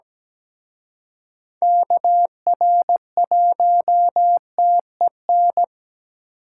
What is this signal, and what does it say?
Morse code, heard as a keyed tone, about 20 wpm, 700 Hz. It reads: KR1TEN
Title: Speed Racing - USA 2x3 Call Signs 17wpm